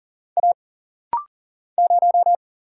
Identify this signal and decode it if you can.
Morse code, heard as a keyed tone, about 39 wpm, 700 Hz. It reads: A 0
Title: Single Letter-Number Alt - Rapid-Fire 40wpm